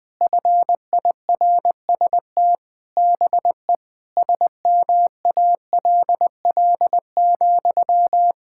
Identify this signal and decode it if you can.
Morse code, heard as a keyed tone, about 20 wpm, 700 Hz. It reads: FIRST BE SMALL,
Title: Sentences from Top 1000 Words - Review Part 6 - 20wpm